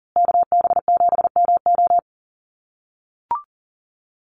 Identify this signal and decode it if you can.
Morse code, heard as a keyed tone, about 40 wpm, 700 Hz. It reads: KB7MO
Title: USA Call Signs Repeated 2x 40wpm